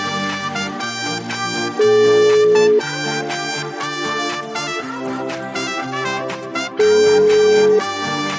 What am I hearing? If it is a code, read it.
Ringback tone (Europe, 425 Hz), with noise.